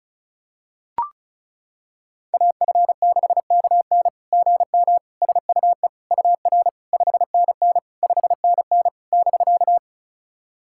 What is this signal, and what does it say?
Morse code, heard as a keyed tone, about 36 wpm, 690 Hz. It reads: AF6KN GM SUE UR 5NN 5NN <BK>
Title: SOTA QSO Practice 35wpm